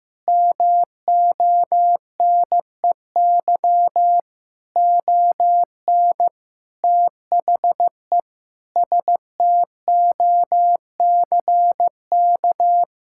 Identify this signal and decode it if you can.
Morse code, heard as a keyed tone, about 15 wpm, 700 Hz. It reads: MONEY ON THE STOCK
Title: Sentences from Top 1000 Words - Review Part 6 - 15wpm